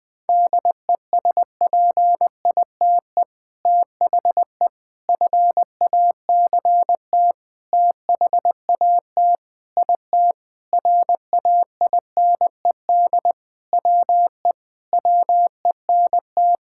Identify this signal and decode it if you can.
Morse code, heard as a keyed tone, about 19 wpm, 700 Hz. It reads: DESPITE THE FACT THAT IT RAINED WE WENT
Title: Sentences with 4-Word Phrases - Spoken Beginning - 20wpm